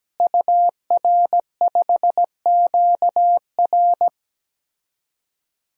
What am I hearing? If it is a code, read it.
Morse code, heard as a keyed tone, about 17 wpm, 700 Hz. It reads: UR5QR
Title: DXCC-100 Call Signs - 5 Characters - Repeated 3x - 17wpm